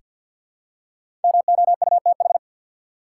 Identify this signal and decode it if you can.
Morse code, heard as a keyed tone, about 48 wpm, 700 Hz. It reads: MOUTH
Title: Five Letter Words - 50wpm